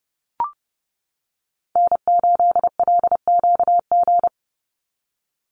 Morse code, heard as a keyed tone, about 30 wpm, 700 Hz. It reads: N8LQG
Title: USA Call Signs 30wpm